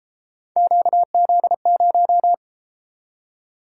Morse code, heard as a keyed tone, about 32 wpm, 700 Hz. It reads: QZ0